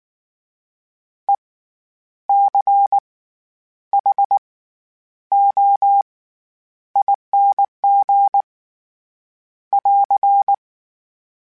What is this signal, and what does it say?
Morse code, heard as a keyed tone, about 19 wpm, spaced as for 9 wpm, 800 Hz. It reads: ECHOING <AR>
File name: Special INGs / Special INGs-mp3s / SlowToFaster ing @18